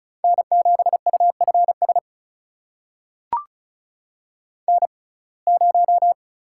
Morse code, heard as a keyed tone, about 36 wpm, spaced as for 25 wpm, 700 Hz. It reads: N7UFS N0
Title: USA Call Signs - 5 Characters - 6x Character Spacing - 35wpm